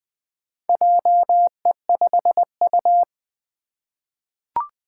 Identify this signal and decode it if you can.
Morse code, heard as a keyed tone, about 20 wpm, 700 Hz. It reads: JE5U
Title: Four Challenging Letter-Number Combinations 20wpm